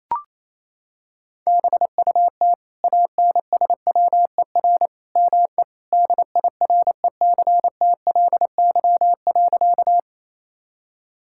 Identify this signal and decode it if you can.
Morse code, heard as a keyed tone, about 28 wpm, 700 Hz. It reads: BUT ANSWER ME DIRECTLY.
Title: Sentences from Top 1000 Words - Review Part 5 - 28wpm